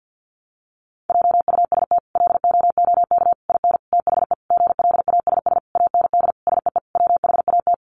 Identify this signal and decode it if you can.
Morse code, heard as a keyed tone, about 50 wpm, 700 Hz. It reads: JUST LOOK IN THE GLASS AND SEE WHAT
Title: Sentences from Top 1300 Words 50wpm